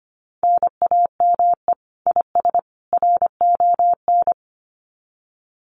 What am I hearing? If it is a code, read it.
Morse code, heard as a keyed tone, about 25 wpm, 700 Hz. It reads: NAME IS RON
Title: Common QSO Phrases 25wpm